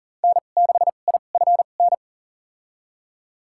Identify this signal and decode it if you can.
Morse code, heard as a keyed tone, about 39 wpm, spaced as for 29 wpm, 700 Hz. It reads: N6IFN
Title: USA Call Signs - 5 Characters - 2x Character Spacing - 40wpm